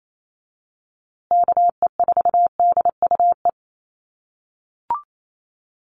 Morse code, heard as a keyed tone, about 28 wpm, 700 Hz. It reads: KE4DUE
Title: USA Call Signs Repeated 2x 28wpm